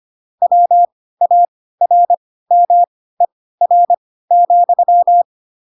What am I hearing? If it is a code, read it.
Morse code, heard as a keyed tone, about 25 wpm, spaced as for 16 wpm, 700 Hz. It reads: WARMER,